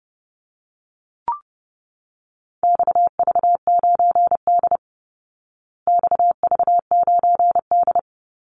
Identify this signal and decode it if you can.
Morse code, heard as a keyed tone, about 30 wpm, 700 Hz. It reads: XV9D XV9D
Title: Call Signs Repeated 3x - 30wpm